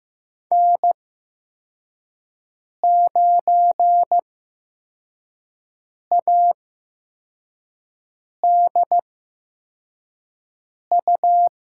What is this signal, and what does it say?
Morse code, heard as a keyed tone, about 15 wpm, spaced as for 4 wpm, 700 Hz. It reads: N9ADU
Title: USA Call Signs - 5 Characters - 8x Character Spacing - 15wpm